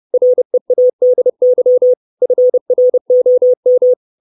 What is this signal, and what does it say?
Morse code, heard as a keyed tone, about 30 wpm, 500 Hz. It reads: READY FROM
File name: 632 words / 632 Words 30WPM